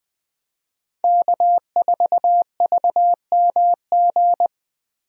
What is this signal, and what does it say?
Morse code, heard as a keyed tone, about 21 wpm, 700 Hz. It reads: K4VMG
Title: USA Call Signs - 5 Characters - Repeated 2x - 20wpm